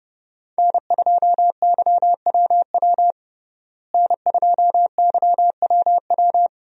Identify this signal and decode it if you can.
Morse code, heard as a keyed tone, about 29 wpm, 700 Hz. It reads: N2YWW N2YWW
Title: USA Call Signs - 5 Characters - Repeated 2x - 30wpm